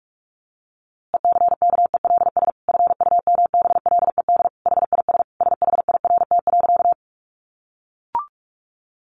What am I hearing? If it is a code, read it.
Morse code, heard as a keyed tone, about 46 wpm, 700 Hz. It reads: ECKELS FUMBLED HIS SHIRT.
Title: A Sound Of Thunder 45wpm